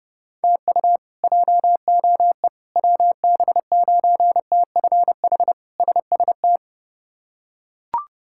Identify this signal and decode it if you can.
Morse code, heard as a keyed tone, about 30 wpm, 700 Hz. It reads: TU JOE WB9TFH SST
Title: K1USN SST QSO Practice - 30wpm